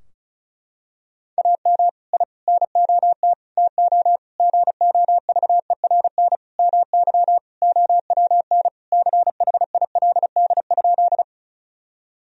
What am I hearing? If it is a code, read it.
Morse code, heard as a keyed tone, about 35 wpm, 700 Hz. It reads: AM I NOT TO GOVERN MY OWN CHILD?
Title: Sentences from Top 1000 Words - Review Part 2 - 35wpm